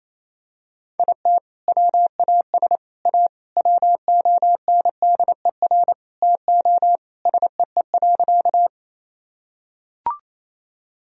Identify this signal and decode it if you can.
Morse code, heard as a keyed tone, about 28 wpm, 700 Hz. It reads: IT WAS A WONDER TO SEE.
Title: Sentences from Top 1000 Words - Review Part 3 - 28wpm